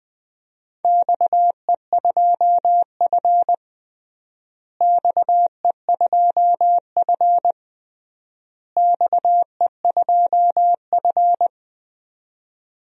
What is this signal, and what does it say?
Morse code, heard as a keyed tone, about 21 wpm, 700 Hz. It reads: XE2F XE2F XE2F
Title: DXCC-100 Call Signs - 4 Characters - Repeated 3x - 20wpm